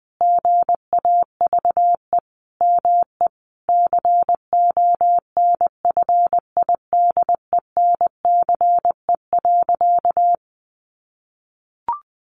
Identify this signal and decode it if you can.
Morse code, heard as a keyed tone, about 20 wpm, 700 Hz. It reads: GAVE ME CONFIDENCE.
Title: Sentences from Top 1500 Words 20wpm